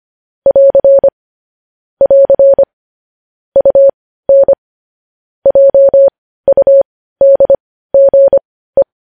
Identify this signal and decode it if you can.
Morse code, heard as a keyed tone, about 25 wpm, spaced as for 15 wpm, 550 Hz. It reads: <AR> <AR> UN JUDGE
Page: news-headlines-in-morse-code-at-15-wpm